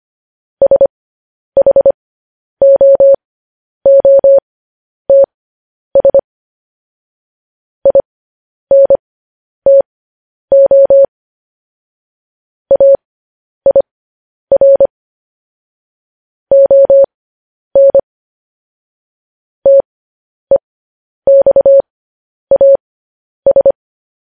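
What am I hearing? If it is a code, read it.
Morse code, heard as a keyed tone, about 25 wpm, spaced as for 10 wpm, 550 Hz. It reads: SHOOTS INTO AIR ON TEXAS
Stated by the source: Morse news headlines practice at 10.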